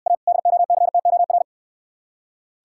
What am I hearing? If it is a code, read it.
Morse code, heard as a keyed tone, about 66 wpm, 700 Hz. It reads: A DOCTOR